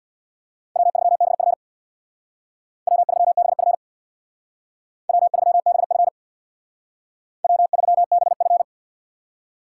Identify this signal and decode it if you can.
Morse code, heard as a keyed tone, about 62 wpm, 700 Hz. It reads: W3BF W3BF W3BF W3BF
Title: Speed Racing - USA 1x2 Call Signs 50wpm